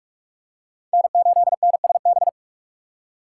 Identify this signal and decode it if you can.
Morse code, heard as a keyed tone, about 46 wpm, 700 Hz. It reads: N8NSB
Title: USA Call Signs - 5 Characters - 3x Character Spacing - 45wpm